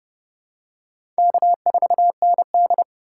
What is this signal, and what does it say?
Morse code, heard as a keyed tone, about 30 wpm, 700 Hz. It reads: K4ND
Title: USA Call Signs - 4 Characters - 3x Character Spacing - 30wpm